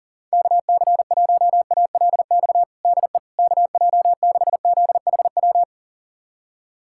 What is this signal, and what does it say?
Morse code, heard as a keyed tone, about 40 wpm, 700 Hz. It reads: KC1ALX DE KJ6ZHW
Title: USA Callsign DE Callsign - 6 Characters - 40wpm